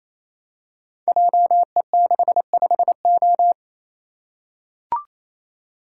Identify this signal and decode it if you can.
Morse code, heard as a keyed tone, about 28 wpm, 700 Hz. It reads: JE65O
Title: Code Groups - 5 Letter:Numbers - 28wpm